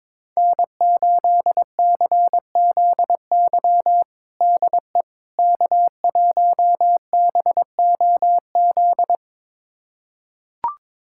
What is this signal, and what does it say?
Morse code, heard as a keyed tone, about 22 wpm, 700 Hz. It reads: N8CZY DE K1BOZ
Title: USA Callsign DE Callsign - 5 Characters - 22wpm